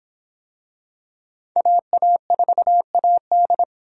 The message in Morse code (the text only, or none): AA4AD